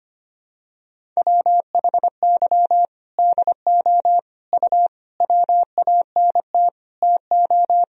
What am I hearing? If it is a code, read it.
Morse code, heard as a keyed tone, about 25 wpm, 700 Hz. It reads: WHY DO U WANT TO
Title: Sentences from Top 200 Words and Abbreviations 25wpm